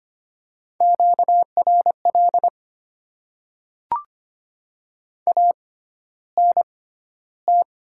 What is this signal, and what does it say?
Morse code, heard as a keyed tone, about 25 wpm, spaced as for 13 wpm, 700 Hz. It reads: QRL ANT
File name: Top 20 QSO Elements - 6x Character Spacing - 25wpm